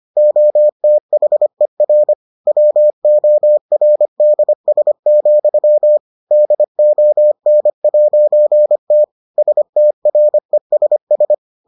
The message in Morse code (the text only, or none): OTHER WORDS, DON'T STRESS